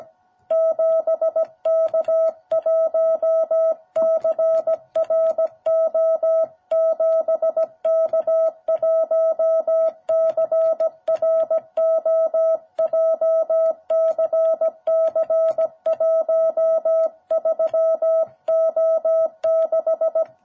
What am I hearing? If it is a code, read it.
Morse code, heard as a keyed tone, about 17 wpm, 660 Hz. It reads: 7K1CRO 7K1CRO JCC13O6
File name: jh8cbh / 2021cw